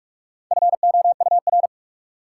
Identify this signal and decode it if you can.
Morse code, heard as a keyed tone, about 46 wpm, 700 Hz. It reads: FOUR
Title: Sentences from Top 300 Words - New Word Drill - 45wpm